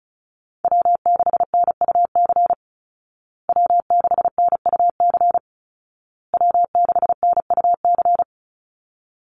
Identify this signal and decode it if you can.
Morse code, heard as a keyed tone, about 34 wpm, 700 Hz. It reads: W6NUC W6NUC W6NUC
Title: Call Signs Repeated 2x - 35wpm